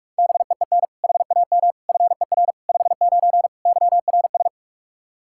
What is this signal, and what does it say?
Morse code, heard as a keyed tone, about 41 wpm, 700 Hz. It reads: BEEN HAM FER 59 YRS
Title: Practice QSO - Exchange 3 - More Info - 45wpm